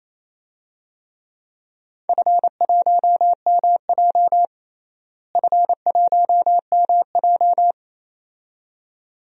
Morse code, heard as a keyed tone, about 28 wpm, 700 Hz. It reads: F1MJ F1MJ
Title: DXCC-100 Call Signs - 4 Characters - Repeated 2x - 28wpm